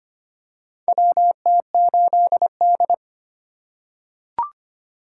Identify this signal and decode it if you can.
Morse code, heard as a keyed tone, about 25 wpm, 700 Hz. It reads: WT8D
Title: USA Call Signs - 4 Characters - 2x Character Spacing - 25wpm